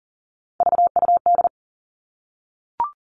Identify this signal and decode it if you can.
Morse code, heard as a keyed tone, about 40 wpm, 700 Hz. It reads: VUD
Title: All Cut Numbers - 3-Digit 40wpm